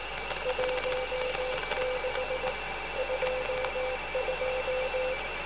Morse code, heard as a keyed tone, about 18 wpm, 510 Hz, with noise. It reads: 16 22